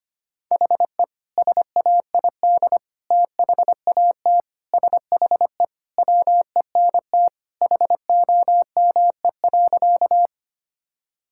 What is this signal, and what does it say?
Morse code, heard as a keyed tone, about 25 wpm, 690 Hz. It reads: HE SAID THAT SHE WENT HOME.
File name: practice / 25wpm / Sentences with 2-Word Phrases - 25wpm